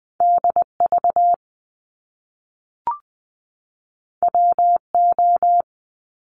Morse code, heard as a keyed tone, about 20 wpm, 700 Hz. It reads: DV WO
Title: Top Two Letter Combinations in Words 20wpm